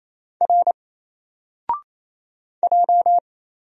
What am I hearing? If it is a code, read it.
Morse code, heard as a keyed tone, about 27 wpm, 700 Hz. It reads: R J
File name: Single-Letter-Number-28wpm